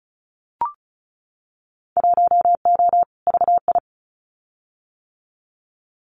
Morse code, heard as a keyed tone, about 36 wpm, 700 Hz. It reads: 1O VI
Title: Winter Field Day Exchanges Encore 35wpm